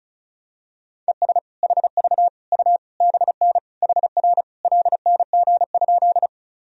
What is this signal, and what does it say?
Morse code, heard as a keyed tone, about 36 wpm, 700 Hz. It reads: ES HV U BN HR LNG?
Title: Sentences from Top 200 Words and Abbreviations 35wpm